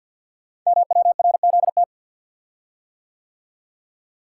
Morse code, heard as a keyed tone, about 48 wpm, 690 Hz. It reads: MWRZT
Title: Code Groups - 5 Letters - 50wpm